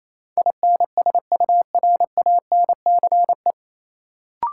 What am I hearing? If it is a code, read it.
Morse code, heard as a keyed tone, about 28 wpm, 700 Hz. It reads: INSURANCE
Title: Sentences from Top 1100 Words - New Word Drill - 28wpm